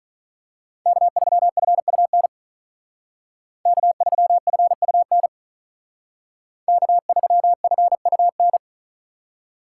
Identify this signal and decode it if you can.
Morse code, heard as a keyed tone, about 41 wpm, 700 Hz. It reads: K3FUN K3FUN K3FUN
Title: Speed Racing - USA 1x3 Call Signs 35wpm